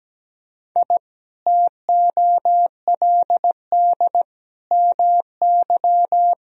Morse code, heard as a keyed tone, about 17 wpm, 700 Hz. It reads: I TOLD MY
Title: Sentences from Top 1000 Words - Review Part 1 - 17wpm